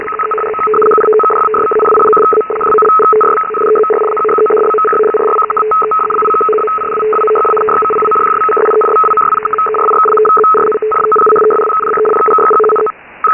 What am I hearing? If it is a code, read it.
Two-tone teleprinter signal, about 850 Hz shift.
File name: Radio / SDR Audio / gqrx